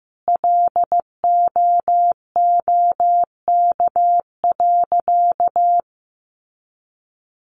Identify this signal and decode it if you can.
Morse code, heard as a keyed tone, about 15 wpm, 700 Hz. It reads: LOOK.
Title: A Sound Of Thunder 15wpm